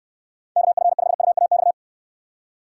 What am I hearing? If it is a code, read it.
Morse code, heard as a keyed tone, about 69 wpm, 700 Hz. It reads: KF5UAX